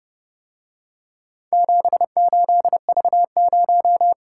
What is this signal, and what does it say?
Morse code, heard as a keyed tone, about 30 wpm, 700 Hz. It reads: 78V0